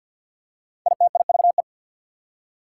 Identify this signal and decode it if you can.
Morse code, heard as a keyed tone, about 53 wpm, 690 Hz. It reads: ITIVE
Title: Prefixes and Suffixes 50wpm